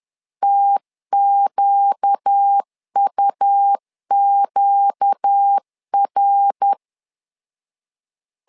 Morse code, heard as a keyed tone, about 10 wpm, 800 Hz. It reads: TQUQR
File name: LEZIONI / 4 Q 10wpm 2210